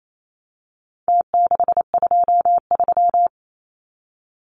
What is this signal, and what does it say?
Morse code, heard as a keyed tone, about 28 wpm, 700 Hz. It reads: T623
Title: Numbers - Contest Style - 4-Digit 28wpm